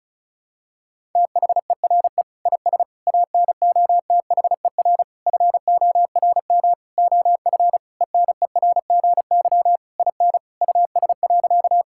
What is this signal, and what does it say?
Morse code, heard as a keyed tone, about 36 wpm, 700 Hz. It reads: THERE IS ANOTHER FORM OF ENERGY IN US.
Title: Sentences from Top 1000 Words - Review Part 6 - 35wpm